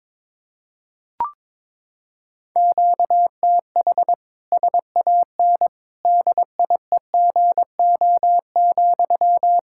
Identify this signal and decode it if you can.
Morse code, heard as a keyed tone, about 22 wpm, 700 Hz. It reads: QTH SAN DIEGO,
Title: Practice QSO - Exchange 1 - Beginning - 22wpm